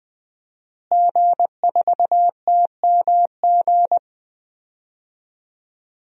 Morse code, heard as a keyed tone, about 21 wpm, 700 Hz. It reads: G4TMG
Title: DXCC-100 Call Signs - 5 Characters - Repeated 2x - 20wpm